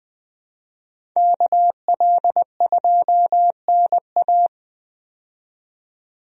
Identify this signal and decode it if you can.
Morse code, heard as a keyed tone, about 20 wpm, 700 Hz. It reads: KL2NA
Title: USA Call Signs - 5 Characters - 20wpm